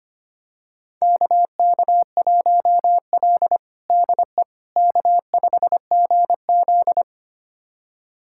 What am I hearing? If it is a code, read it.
Morse code, heard as a keyed tone, about 25 wpm, 700 Hz. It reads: KK1L DE K5GZ
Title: USA Callsign DE Callsign - 4 Characters - 25wpm